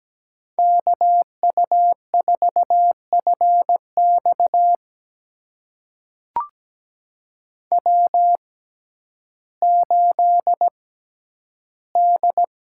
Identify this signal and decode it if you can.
Morse code, heard as a keyed tone, about 17 wpm, spaced as for 10 wpm, 700 Hz. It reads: KU4FX W8D
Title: USA Call Signs - 5 Characters - 6x Character Spacing - 17wpm